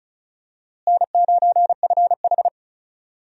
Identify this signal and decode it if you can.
Morse code, heard as a keyed tone, about 36 wpm, 700 Hz. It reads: N9FH
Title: USA Call Signs - 4 Characters - Repeated 3x - 35wpm